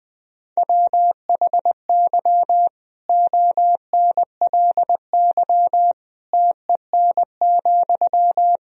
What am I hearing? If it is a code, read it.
Morse code, heard as a keyed tone, about 20 wpm, 700 Hz. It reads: WHY ONLY TEN,
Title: Sentences from Top 1000 Words - Review Part 4 - 20wpm